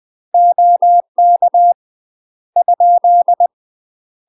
Morse code, heard as a keyed tone, about 20 wpm, 700 Hz. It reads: OK ?
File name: UK MASTER QSOs TRAINING / 20WPM / UK MASTER QSOs 0013